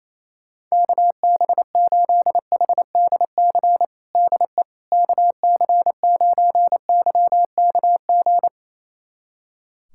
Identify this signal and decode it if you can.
Morse code, heard as a keyed tone, about 28 wpm, 700 Hz. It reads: KB8HDC DE KC9YKG
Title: USA Callsign DE Callsign - 6 Characters - 28wpm